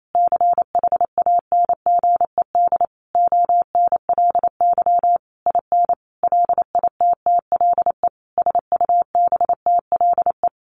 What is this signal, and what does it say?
Morse code, heard as a keyed tone, about 28 wpm, 700 Hz. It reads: CHANGED ONLY IN LITTLE SUBTLE
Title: A Sound Of Thunder 28wpm